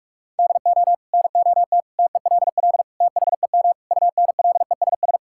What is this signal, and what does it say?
Morse code, heard as a keyed tone, about 47 wpm, 700 Hz. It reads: DO NOT TELL THEM UNLESS
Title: Sentences from Top 1000 Words - Review Part 6 - 45wpm